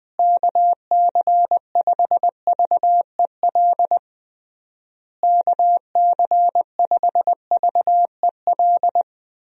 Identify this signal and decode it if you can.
Morse code, heard as a keyed tone, about 20 wpm, 700 Hz. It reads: KC5VEL KC5VEL
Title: USA Call Signs - 6 Characters - Repeated 3x - 20wpm